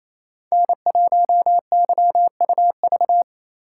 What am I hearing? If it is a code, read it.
Morse code, heard as a keyed tone, about 28 wpm, 700 Hz. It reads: N1YUV
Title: USA Call Signs - 5 Characters - 8x Character Spacing - 28wpm